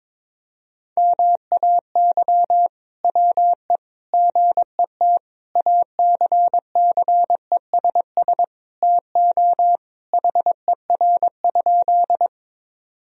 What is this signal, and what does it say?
Morse code, heard as a keyed tone, about 21 wpm, 700 Hz. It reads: MAY WE GET ACCESS TO HER?
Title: Sentences from Top 1000 Words - Review Part 4 - 22wpm